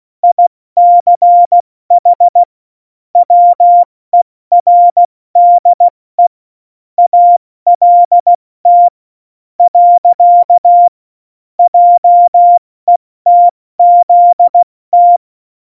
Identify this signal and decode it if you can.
Morse code, heard as a keyed tone, about 16 wpm, spaced as for 13 wpm, 700 Hz. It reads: ICH WERDE ALT. JETZT
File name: Morsekurs / Witze / Witz 14. August 2018, 70 F80